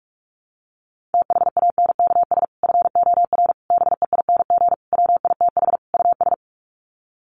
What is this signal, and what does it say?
Morse code, heard as a keyed tone, about 44 wpm, 700 Hz. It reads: THANKS FOR BEING WITH US
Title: Sets of 5 Words 45wpm